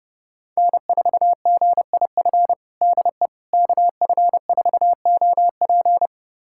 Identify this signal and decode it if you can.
Morse code, heard as a keyed tone, about 30 wpm, 700 Hz. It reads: N4GIF DE KF4OP
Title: USA Callsign DE Callsign - 5 Characters - 30wpm